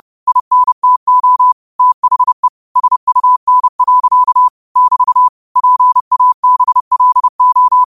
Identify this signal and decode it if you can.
Morse code, heard as a keyed tone, about 30 wpm, 1000 Hz. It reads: INTO THE SUN. <BT> PABLO
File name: qotdfiles / 30wpm